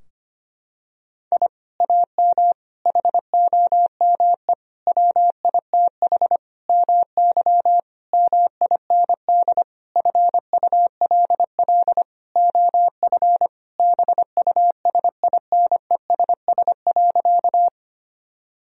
Morse code, heard as a keyed tone, about 25 wpm, 700 Hz. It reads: I AM HOME WITH MY MIND FULL OF BUSINESS.
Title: Sentences from Top 1000 Words - Review Part 3 - 25wpm